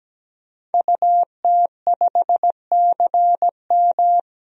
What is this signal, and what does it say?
Morse code, heard as a keyed tone, about 17 wpm, 700 Hz. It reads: UT5CM
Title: DXCC-100 Call Signs - 5 Characters - 17wpm